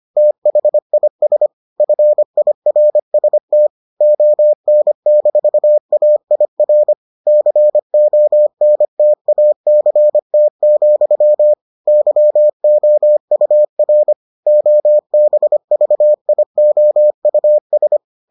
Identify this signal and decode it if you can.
Morse code, heard as a keyed tone, about 25 wpm, 600 Hz. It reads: THIS FIRST ON-AIR CONTACT, YOUR OBVIOUS